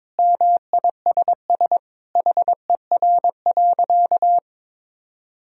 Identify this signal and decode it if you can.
Morse code, heard as a keyed tone, about 22 wpm, 700 Hz. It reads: MISS HER.
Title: Sentences from Top 1000 Words - Review Part 3 - 22wpm